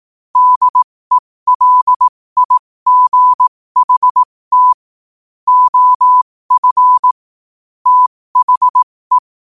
Morse code, heard as a keyed tone, about 18 wpm, spaced as for 15 wpm, 1000 Hz. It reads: DELIGHT OF THE
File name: qotdfiles / 15wpm